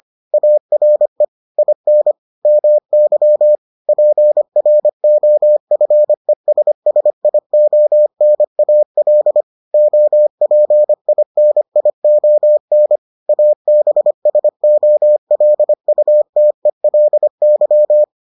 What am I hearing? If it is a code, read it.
Morse code, heard as a keyed tone, about 25 wpm, 600 Hz. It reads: ARE IN MY PROFESSIONAL OPINION ABSOLUTELY